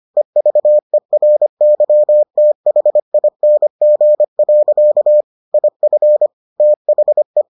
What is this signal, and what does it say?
Morse code, heard as a keyed tone, about 25 wpm, 600 Hz. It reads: EVERYTHING. IF THE